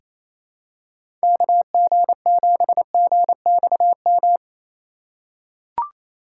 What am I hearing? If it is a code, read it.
Morse code, heard as a keyed tone, about 28 wpm, 700 Hz. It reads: KG7GXM
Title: USA Call Signs - 6 Characters - 3x Character Spacing - 28wpm